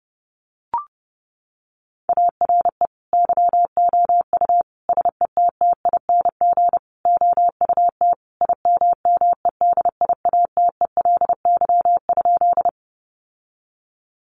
Morse code, heard as a keyed tone, about 30 wpm, 700 Hz. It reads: ARE YOU SETTING OUT IMMEDIATELY?
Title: Sentences from Top 1300 Words 30wpm